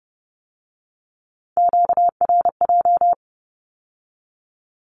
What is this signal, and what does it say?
Morse code, heard as a keyed tone, about 30 wpm, 700 Hz. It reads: QRJ
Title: Q-Codes with Spoken Meaning - 30wpm